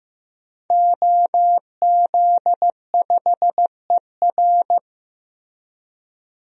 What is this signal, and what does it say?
Morse code, heard as a keyed tone, about 15 wpm, 700 Hz. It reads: OZ5ER
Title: DXCC-100 Call Signs - 5 Characters - Repeated 3x - 15wpm